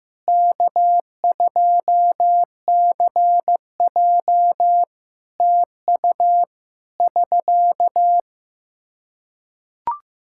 Morse code, heard as a keyed tone, about 15 wpm, 700 Hz. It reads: K2CJ TU <SK>
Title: K1USN SST QSO Practice 15wpm